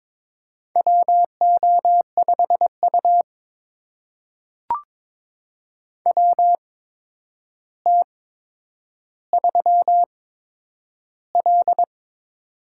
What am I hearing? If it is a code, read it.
Morse code, heard as a keyed tone, about 22 wpm, spaced as for 9 wpm, 700 Hz. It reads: WO5U WT3L
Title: USA Call Signs - 4 Characters - 8x Character Spacing - 22wpm